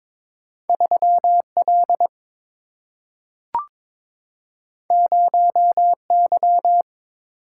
Morse code, heard as a keyed tone, about 22 wpm, 700 Hz. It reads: 3L 0Y
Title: Single-Letter Single-Number 22wpm